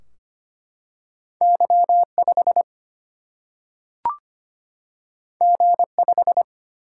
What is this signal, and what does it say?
Morse code, heard as a keyed tone, about 25 wpm, 700 Hz. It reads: Y5 G5
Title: Single-Letter Single-Number 25wpm